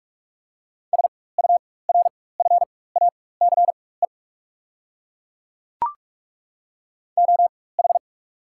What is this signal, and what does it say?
Morse code, heard as a keyed tone, about 42 wpm, spaced as for 21 wpm, 690 Hz. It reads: SURFACE OH
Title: 4x Character Spacing - Top 500 Words - 45wpm